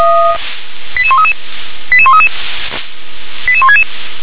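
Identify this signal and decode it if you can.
Five-tone selective call (ZVEI), heard as quick runs of five tones, tones of about 70 ms. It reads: 88133 88133 88166
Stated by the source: ZVEI five-tone selective call received with an SDR.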